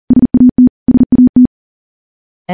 Morse code, heard as a keyed tone, about 40 wpm, 250 Hz. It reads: SAT SAT
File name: morse / morse-training-40WPM